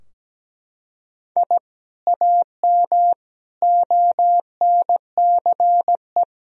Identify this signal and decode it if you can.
Morse code, heard as a keyed tone, about 17 wpm, 700 Hz. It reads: I AM ONCE
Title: Sentences from Top 1000 Words - Review Part 3 - 17wpm